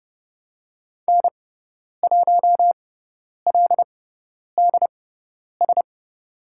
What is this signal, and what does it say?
Morse code, heard as a keyed tone, about 31 wpm, spaced as for 10 wpm, 700 Hz. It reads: N1LDS
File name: Lesson 17g - Letter u - Call Signs 30wpm